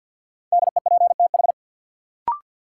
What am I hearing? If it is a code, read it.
Morse code, heard as a keyed tone, about 49 wpm, 700 Hz. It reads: DEPTH
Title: Sentences from Top 1200 Words - New Word Drill - 50wpm